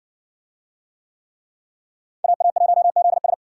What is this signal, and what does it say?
Morse code, heard as a keyed tone, about 55 wpm, 700 Hz. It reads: AA1ZS